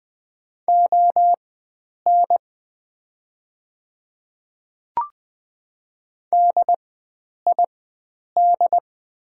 Morse code, heard as a keyed tone, about 20 wpm, spaced as for 9 wpm, 700 Hz. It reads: ON DID
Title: 4x Character Spacing - Top 100 Words - 20wpm